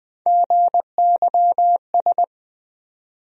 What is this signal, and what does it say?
Morse code, heard as a keyed tone, about 21 wpm, 700 Hz. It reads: GYS
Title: USA Call Sign Postfixes - 3 Letters 20wpm